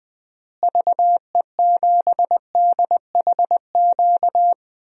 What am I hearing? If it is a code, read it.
Morse code, heard as a keyed tone, about 21 wpm, 700 Hz. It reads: VE7DHQ
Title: DXCC-100 Call Signs - 6 Characters - Repeated 3x - 20wpm